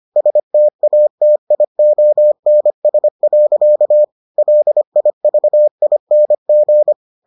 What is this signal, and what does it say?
Morse code, heard as a keyed tone, about 25 wpm, 600 Hz. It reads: STATIONS. LIVING